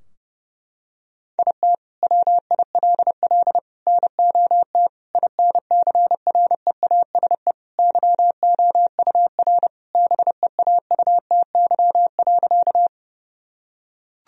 Morse code, heard as a keyed tone, about 32 wpm, 700 Hz. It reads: IT WILL NOT INCREASE YOUR BEAUTY.
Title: Sentences from Top 1000 Words - Review Part 4 - 30wpm